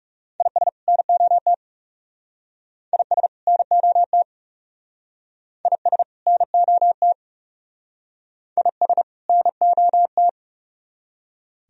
Morse code, heard as a keyed tone, about 36 wpm, 700 Hz. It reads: IS NOT IS NOT IS NOT IS NOT
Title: Speed-Racing - Sets of 2 Words 30wpm